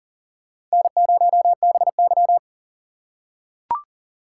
Morse code, heard as a keyed tone, about 40 wpm, 700 Hz. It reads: N0BY
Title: USA Call Signs - 4 Characters - 4x Character Spacing - 40wpm